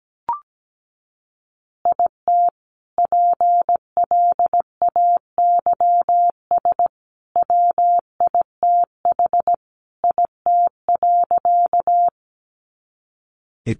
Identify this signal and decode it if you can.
Morse code, heard as a keyed tone, about 17 wpm, 700 Hz. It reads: IT PLAYS WITH IT.
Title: Additional Easy Phrases 17wpm